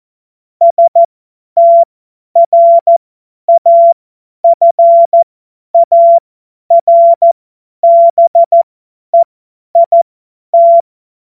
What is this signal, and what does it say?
Morse code, heard as a keyed tone, about 14 wpm, spaced as for 10 wpm, 680 Hz. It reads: STRAFARBEIT
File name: Morsekurs / Witze / Witz 13. September 2022 50 F70